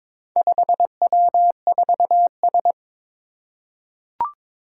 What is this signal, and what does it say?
Morse code, heard as a keyed tone, about 22 wpm, 700 Hz. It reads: 5W4S
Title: Four Challenging Letter-Number Combinations 22wpm